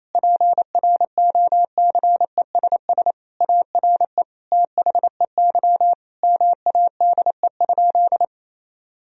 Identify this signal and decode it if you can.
Morse code, heard as a keyed tone, about 28 wpm, 700 Hz. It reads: PROCESS ARE THEY MADE?
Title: Sentences from Top 1000 Words - Review Part 3 - 28wpm